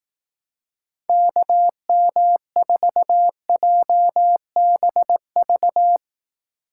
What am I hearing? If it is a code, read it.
Morse code, heard as a keyed tone, about 19 wpm, 700 Hz. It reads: KM4JBV
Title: Speed Racing - USA 2x3 Call Signs 15wpm